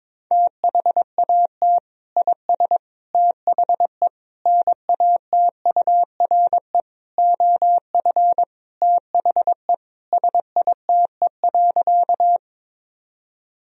Morse code, heard as a keyed tone, about 22 wpm, 700 Hz. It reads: THAT IS THE NATURE OF THE SITE.
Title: Sentences from Top 1000 Words - Review Part 4 - 22wpm